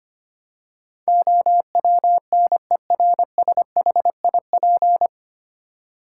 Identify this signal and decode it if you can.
Morse code, heard as a keyed tone, about 25 wpm, 700 Hz. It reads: OWNERSHIP
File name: Nine Letter Words - 25wpm